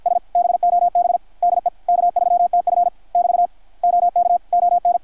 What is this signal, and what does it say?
Morse code, heard as a keyed tone, about 53 wpm, 700 Hz, with radio noise. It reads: S BOB DE K3TF <BT> OK ON
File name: archived / CRSnet